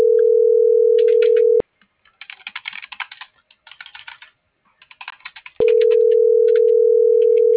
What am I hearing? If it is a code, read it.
Ringback tone (North America), 440 + 480 Hz.